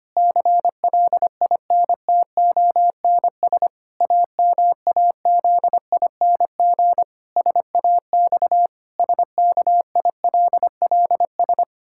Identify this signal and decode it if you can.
Morse code, heard as a keyed tone, about 25 wpm, 700 Hz. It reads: CLINTONS AMAZING SAX SKILLS
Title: Pangram Challenge v2 - 25wpm